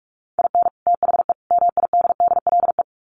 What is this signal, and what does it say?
Morse code, heard as a keyed tone, about 44 wpm, 700 Hz. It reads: IN THE MIDDLE
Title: Sets of 3 Words 45wpm